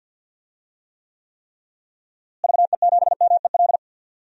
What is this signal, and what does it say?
Morse code, heard as a keyed tone, about 51 wpm, 700 Hz. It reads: VE7MEL
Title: DXCC-100 Call Signs - 6 Characters - Repeated 3x - 50wpm